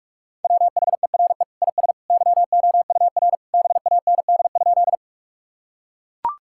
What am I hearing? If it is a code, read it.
Morse code, heard as a keyed tone, about 41 wpm, 700 Hz. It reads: WHERE IS YOUR BAND?
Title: Sentences from Top 1600 Words 45wpm